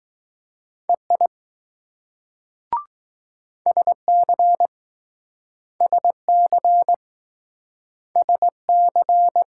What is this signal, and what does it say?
Morse code, heard as a keyed tone, about 20 wpm, 700 Hz. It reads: EI SC SC SC